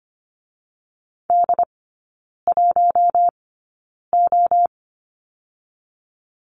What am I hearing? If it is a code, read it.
Morse code, heard as a keyed tone, about 25 wpm, spaced as for 9 wpm, 690 Hz. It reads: D1O
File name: Lesson 14e - Number 2 - Three Characters 25wpm